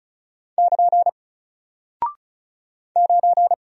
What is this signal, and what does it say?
Morse code, heard as a keyed tone, about 36 wpm, 700 Hz. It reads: <KN> 9
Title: Single Character (Letter:Number:Punctuation:Prosign) Alt 35wpm